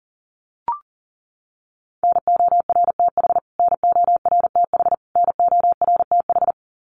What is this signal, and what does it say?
Morse code, heard as a keyed tone, about 40 wpm, 700 Hz. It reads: NORTH NORTH NORTH
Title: Top 500 Words Repeated 40wpm